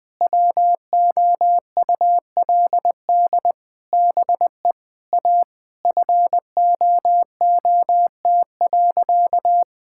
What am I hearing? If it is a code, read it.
Morse code, heard as a keyed tone, about 20 wpm, 700 Hz. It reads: WOULD BE A FOOT.
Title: All-Age Appropriate Jokes - 20wpm